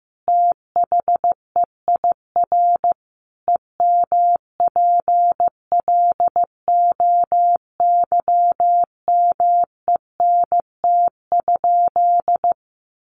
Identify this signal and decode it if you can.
Morse code, heard as a keyed tone, about 15 wpm, 700 Hz. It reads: THEIR EMPLOYMENT?
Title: Sentences from Top 1500 Words 15wpm